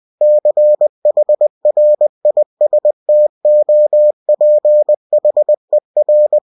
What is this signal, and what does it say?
Morse code, heard as a keyed tone, about 20 wpm, 600 Hz. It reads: CHRISTOPHER
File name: cw-books / winnie-the-pooh / winnie-20wpm-page-0080